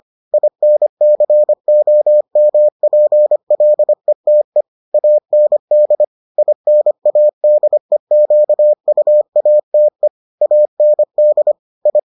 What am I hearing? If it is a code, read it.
Morse code, heard as a keyed tone, about 25 wpm, 600 Hz. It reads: INCOMPLETE AND INADEQUATE AND I